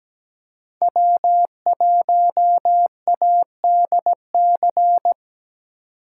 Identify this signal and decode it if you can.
Morse code, heard as a keyed tone, about 20 wpm, 700 Hz. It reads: W1ADC
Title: Speed Racing - USA 1x3 Call Signs 17wpm